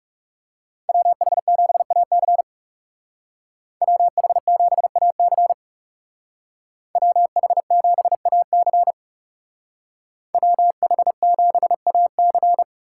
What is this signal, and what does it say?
Morse code, heard as a keyed tone, about 39 wpm, 700 Hz. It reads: WH7AC WH7AC WH7AC WH7AC